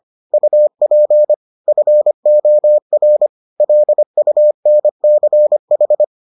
Morse code, heard as a keyed tone, about 25 wpm, 600 Hz. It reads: UP FOR LUNCH